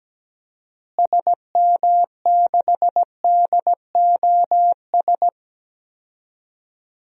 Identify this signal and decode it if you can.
Morse code, heard as a keyed tone, about 17 wpm, 700 Hz. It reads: SM6DOS
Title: DXCC-100 Call Signs - 6 Characters - Repeated 2x - 17wpm